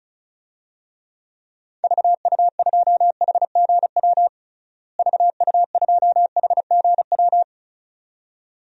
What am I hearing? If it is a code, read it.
Morse code, heard as a keyed tone, about 36 wpm, 700 Hz. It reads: VU2HGW VU2HGW
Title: DXCC-100 Call Signs - 6 Characters - Repeated 2x - 35wpm